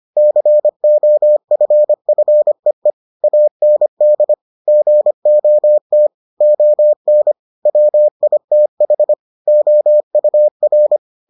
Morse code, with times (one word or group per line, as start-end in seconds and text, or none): 0.2-2.9 COFFEE
3.2-4.3 AND
4.7-6.1 GOT
6.4-7.3 ON
7.7-9.1 WITH
9.5-11.0 OUR